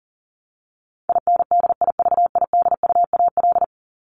Morse code, heard as a keyed tone, about 40 wpm, 700 Hz. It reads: INDIVIDUAL
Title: ES:S Suffix - Top 100 Words with Base Words 40wpm